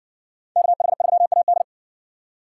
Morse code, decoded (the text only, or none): KH3AD